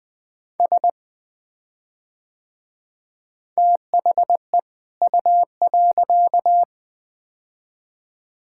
Morse code, heard as a keyed tone, about 20 wpm, 700 Hz. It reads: S THE U.
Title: Build A Phrase - 2-Word Phrases - 20wpm